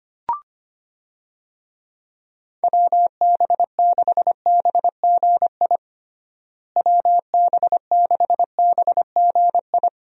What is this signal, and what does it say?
Morse code, heard as a keyed tone, about 25 wpm, 700 Hz. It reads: WB6BGI WB6BGI
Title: USA Call Signs - 6 Characters - Repeated 2x - 25wpm